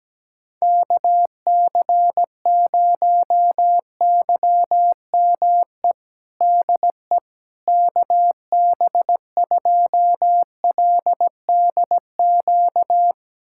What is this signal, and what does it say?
Morse code, heard as a keyed tone, about 17 wpm, 700 Hz. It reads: KC0YME DE KB2LDQ
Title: USA Callsign DE Callsign - 6 Characters - 17wpm